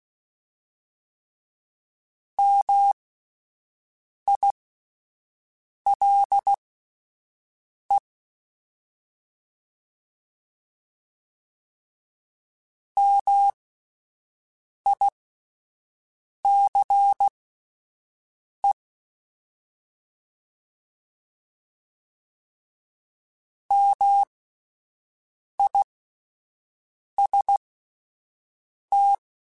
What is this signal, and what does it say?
Morse code, heard as a keyed tone, about 16 wpm, spaced as for 5 wpm, 800 Hz. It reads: MILE MICE MIST
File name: Alphabetical / Alphabetical mp3s / Ms @5